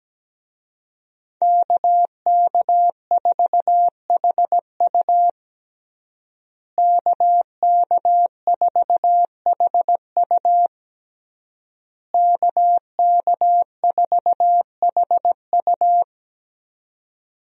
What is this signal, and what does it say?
Morse code, heard as a keyed tone, about 17 wpm, 700 Hz. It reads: KK4HU KK4HU KK4HU
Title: USA Call Signs - 5 Characters - Repeated 3x - 17wpm